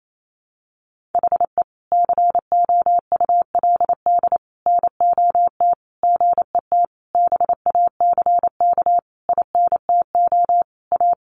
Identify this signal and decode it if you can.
Morse code, heard as a keyed tone, about 28 wpm, 700 Hz. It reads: HE COULD NOT GET BACK INTO A
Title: Sentences from Top 1100 Words 28wpm